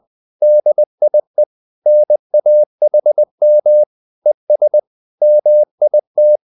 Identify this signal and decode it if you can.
Morse code, heard as a keyed tone, about 20 wpm, 600 Hz. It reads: DIE NAHM ES MIT